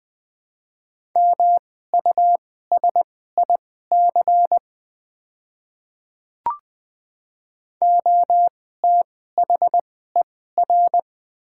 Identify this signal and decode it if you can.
Morse code, heard as a keyed tone, about 20 wpm, spaced as for 15 wpm, 700 Hz. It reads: MUSIC OTHER
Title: 2x Character Spacing - Top 500 Words - 20wpm